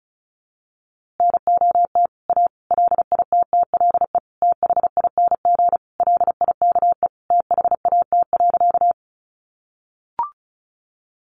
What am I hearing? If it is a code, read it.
Morse code, heard as a keyed tone, about 34 wpm, 700 Hz. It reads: NOT A LITTLE THING LIKE THAT.
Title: A Sound Of Thunder 35wpm